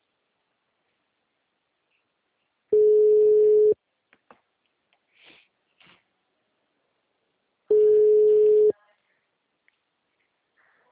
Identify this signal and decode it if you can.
Ringback tone (Europe, 425 Hz).
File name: out- -2017- - - .6